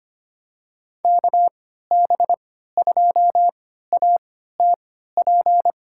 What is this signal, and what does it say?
Morse code, heard as a keyed tone, about 25 wpm, spaced as for 14 wpm, 700 Hz. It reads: KB2ATP
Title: USA Call Signs - 6 Characters - 3x Character Spacing - 25wpm